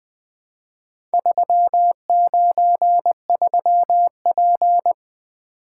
Morse code, heard as a keyed tone, about 20 wpm, 700 Hz. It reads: 393P
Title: Four Letter-Number Combinations - 20wpm